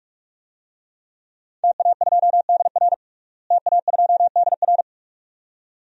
Morse code, heard as a keyed tone, about 46 wpm, 700 Hz. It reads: TA2DR TA2DR
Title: DXCC-100 Call Signs - 5 Characters - Repeated 2x - 45wpm